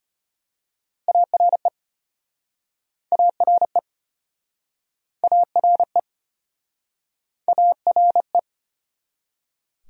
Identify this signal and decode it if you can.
Morse code, heard as a keyed tone, about 33 wpm, 700 Hz. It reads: ARE ARE ARE ARE